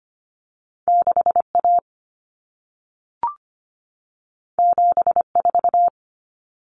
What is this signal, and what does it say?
Morse code, heard as a keyed tone, about 25 wpm, 700 Hz. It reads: 6A 74
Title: Common Cut Numbers - 2-Digit 25wpm